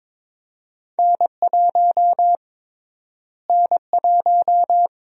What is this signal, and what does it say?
Morse code, heard as a keyed tone, about 22 wpm, 700 Hz. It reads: N1 N1
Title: USA Call Sign Prefixes - Repeated 2x - 22wpm